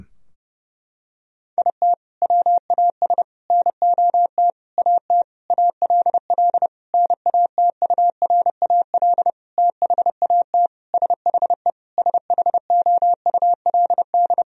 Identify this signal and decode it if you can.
Morse code, heard as a keyed tone, about 32 wpm, 700 Hz. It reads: IT WAS NOT AT ALL NATURAL THAT SHE SHOULD
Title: Sentences from Top 1000 Words - Review Part 4 - 30wpm